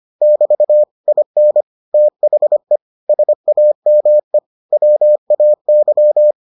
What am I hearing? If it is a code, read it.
Morse code, heard as a keyed tone, about 25 wpm, 600 Hz. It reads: <BT> IN THE SAME WAY